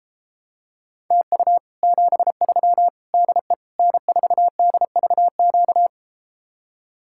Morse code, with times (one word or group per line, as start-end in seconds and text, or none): 1.1-1.6 TU
1.8-2.9 73
3.1-3.5 DE
3.8-5.9 N4DVQ